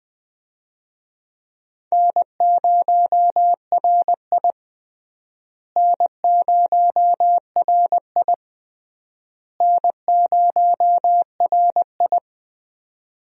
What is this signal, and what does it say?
Morse code, heard as a keyed tone, about 21 wpm, 700 Hz. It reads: N0RI N0RI N0RI
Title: USA Call Signs - 4 Characters - Repeated 3x - 20wpm